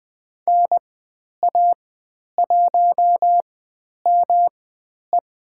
Morse code, heard as a keyed tone, about 20 wpm, spaced as for 10 wpm, 700 Hz. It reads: NA1ME
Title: 109 Lesson 18g - Letter m - Call Signs 20wpm